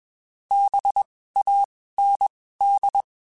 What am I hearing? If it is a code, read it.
Morse code, heard as a keyed tone, about 21 wpm, spaced as for 15 wpm, 800 Hz. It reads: BAND
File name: (spaced) / QSOs Spaced-mp3s / WD0BGP 1 @20 sp